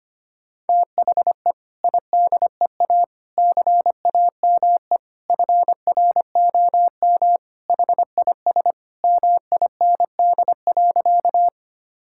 Morse code, heard as a keyed tone, about 25 wpm, 700 Hz. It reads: THE IDEA CAME FROM HIS MIND.
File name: practice / 25wpm / Sentences with 2-Word Phrases - 25wpm